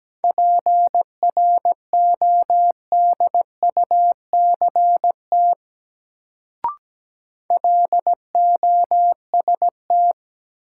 Morse code, heard as a keyed tone, about 17 wpm, 700 Hz. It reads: PRODUCT LOST
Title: Sentences from Top 1400 Words - New Word Drill Alt - 17wpm